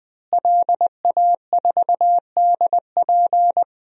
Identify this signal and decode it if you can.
Morse code, heard as a keyed tone, about 21 wpm, 700 Hz. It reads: LA4DP
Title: DXCC-100 Call Signs - 5 Characters - Repeated 2x - 20wpm